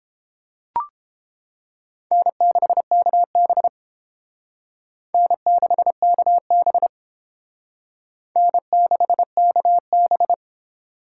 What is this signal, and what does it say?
Morse code, heard as a keyed tone, about 32 wpm, 700 Hz. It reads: N6KB N6KB N6KB